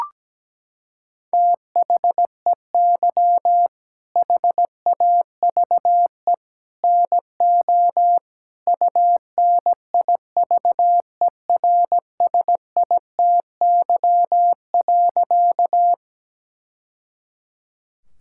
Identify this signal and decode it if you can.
Morse code, heard as a keyed tone, about 17 wpm, 700 Hz. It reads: THEY HAVE NO UNIVERSITY.
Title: Sentences from Top 1000 Words - Review Part 5 - 17wpm